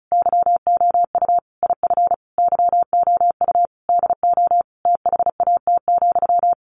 Morse code, heard as a keyed tone, about 34 wpm, 700 Hz. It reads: YOU IF YOU DO THAT,
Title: Sentences from Top 1500 Words 35wpm